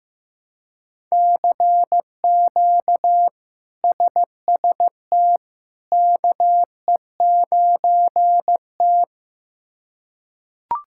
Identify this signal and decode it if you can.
Morse code, heard as a keyed tone, about 15 wpm, 700 Hz. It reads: CQ SST KE9T
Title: K1USN SST QSO Practice - 15wpm